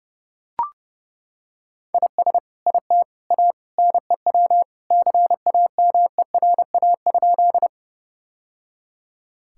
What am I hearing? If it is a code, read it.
Morse code, heard as a keyed tone, about 30 wpm, 700 Hz. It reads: IS IT A NEW CAMERA?
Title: Sentences from Top 1000 Words - Review Part 5 - 30wpm